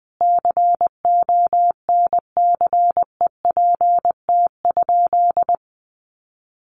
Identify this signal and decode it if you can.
Morse code, heard as a keyed tone, about 19 wpm, 700 Hz. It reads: CONCEPT?
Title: Sentences from Top 1100 Words 20wpm